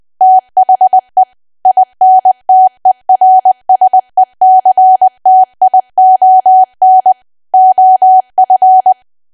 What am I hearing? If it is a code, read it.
Morse code, heard as a keyed tone, about 20 wpm, 750 Hz. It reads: THE INTERSECTION OF
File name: cwa / W1AW-Level3 / 20WPM